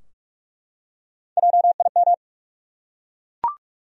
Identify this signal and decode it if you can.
Morse code, heard as a keyed tone, about 43 wpm, 700 Hz. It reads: JIM
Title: USA Common Names - 45wpm